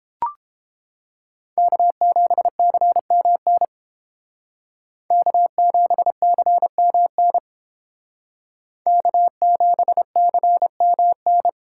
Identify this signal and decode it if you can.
Morse code, heard as a keyed tone, about 30 wpm, 700 Hz. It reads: K7CMN K7CMN K7CMN